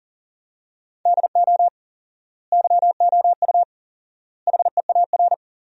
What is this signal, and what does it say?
Morse code, heard as a keyed tone, about 40 wpm, 700 Hz. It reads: DO YOU HEAR
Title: 4x Word-Spacing - Sentences from Top 300 Words - 40wpm